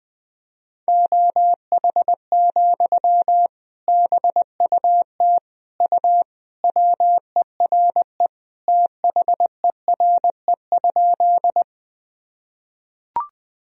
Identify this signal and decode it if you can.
Morse code, heard as a keyed tone, about 21 wpm, 700 Hz. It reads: OH, BUT U WERE THERE?
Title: Sentences from Top 500 Words and Abbreviations 20wpm